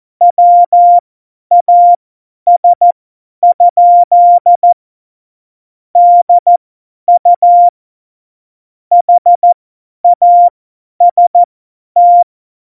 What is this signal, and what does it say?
Morse code, heard as a keyed tone, about 14 wpm, spaced as for 10 wpm, 700 Hz. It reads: WAS? DU HAST
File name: Morsekurs / Witze / Witz 07 Juli 2020, 50 F70